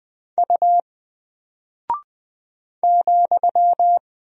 Morse code, heard as a keyed tone, about 20 wpm, 700 Hz. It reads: U ,
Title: Single Character (Letter:Number:Punctuation:Prosign) 20wpm